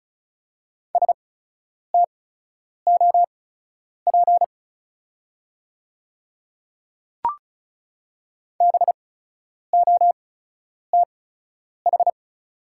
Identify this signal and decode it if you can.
Morse code, heard as a keyed tone, about 36 wpm, spaced as for 10 wpm, 700 Hz. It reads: STOP BOTH
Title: 8x Character Spacing - Top 300 Words - 35wpm